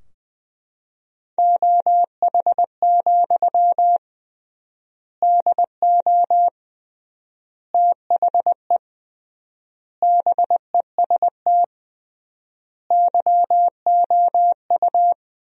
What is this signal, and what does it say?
Morse code, heard as a keyed tone, about 20 wpm, 700 Hz. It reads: OH, DO THE BEST YOU
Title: 3x Word-Spacing - Sentences from Top 500 Words - 20wpm